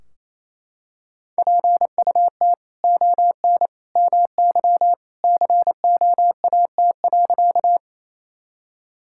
Morse code, heard as a keyed tone, about 28 wpm, 700 Hz. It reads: PUT ON MY COAT.
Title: Sentences from Top 1600 Words 28wpm